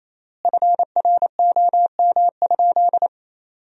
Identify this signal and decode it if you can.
Morse code, heard as a keyed tone, about 28 wpm, 700 Hz. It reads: FROM?
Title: 3x Word-Spacing - Sentences from Top 400 Words - 28wpm